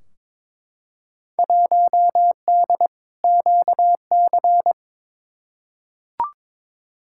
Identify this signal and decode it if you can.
Morse code, heard as a keyed tone, about 22 wpm, 700 Hz. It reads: 1D QC
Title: Field Day Exchanges Encore - 22wpm